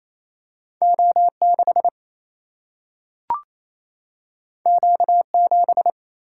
Morse code, heard as a keyed tone, about 28 wpm, 700 Hz. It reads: O6 Q7
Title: Single-Letter Single-Number 28wpm